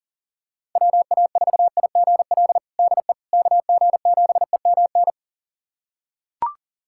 Morse code, heard as a keyed tone, about 40 wpm, 700 Hz. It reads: WA4IGL DE KG7EMN
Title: USA Callsign DE Callsign - 6 Characters - 40wpm